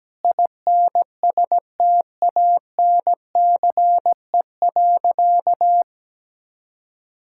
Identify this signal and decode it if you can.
Morse code, heard as a keyed tone, about 17 wpm, 700 Hz. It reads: INSTANCE.
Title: Sentences from Top 1000 Words - Review Part 5 - 17wpm